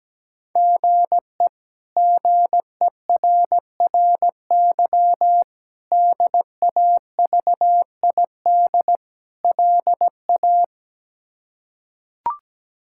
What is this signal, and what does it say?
Morse code, heard as a keyed tone, about 17 wpm, 700 Hz. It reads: GE GERRY DAVID LA
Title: K1USN SST QSO Practice - 17wpm